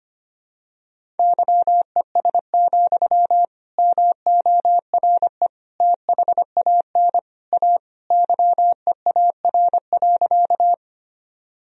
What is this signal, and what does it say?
Morse code, heard as a keyed tone, about 25 wpm, 700 Hz. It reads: YES, MORE THAN A YEAR.
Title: Sentences from Top 1000 Words - Review Part 1 - 25wpm